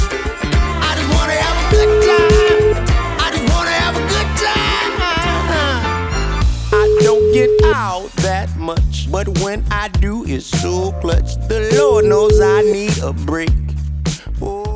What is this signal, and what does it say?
Ringback tone (Europe, 425 Hz), with noise.